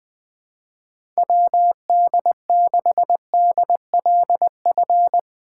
Morse code, heard as a keyed tone, about 20 wpm, 700 Hz. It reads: WD6DLF
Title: USA Call Signs - 6 Characters - 8x Character Spacing - 20wpm